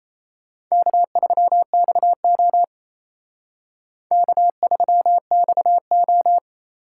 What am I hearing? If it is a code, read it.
Morse code, heard as a keyed tone, about 34 wpm, 700 Hz. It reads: K3XO K3XO
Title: Speed Racing - USA 1x2 Call Signs 28wpm